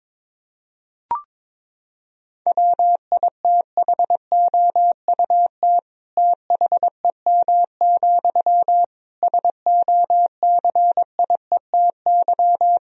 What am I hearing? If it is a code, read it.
Morse code, heard as a keyed tone, about 22 wpm, 700 Hz. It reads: WITHOUT THEM, SOCIETY
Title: Sentences from Top 1000 Words - Review Part 4 - 22wpm